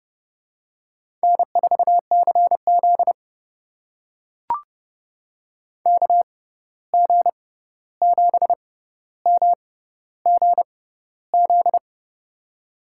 Morse code, heard as a keyed tone, about 32 wpm, spaced as for 14 wpm, 700 Hz. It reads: N4CZ KG7MGZ
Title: USA Call Signs - 6x Character Spacing - 30wpm